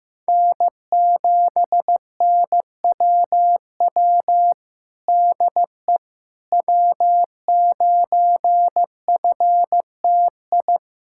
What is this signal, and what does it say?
Morse code, heard as a keyed tone, about 15 wpm, 700 Hz. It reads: N7NWW DE W9FTI
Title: Practice QSO - Exchange 4 - Ending - 15wpm